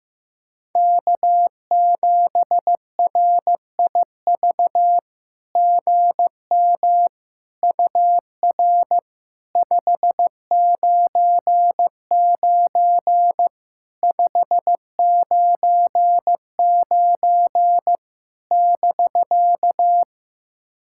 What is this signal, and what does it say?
Morse code, heard as a keyed tone, about 14 wpm, 700 Hz. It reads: K7RIV GM UR 599 599 <BK>
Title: SOTA QSO Practice 15wpm